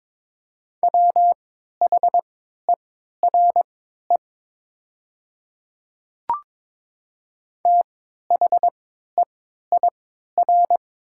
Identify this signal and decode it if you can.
Morse code, heard as a keyed tone, about 22 wpm, spaced as for 13 wpm, 700 Hz. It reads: WHERE THEIR
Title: 3x Character Spacing - Top 300 Words - 22wpm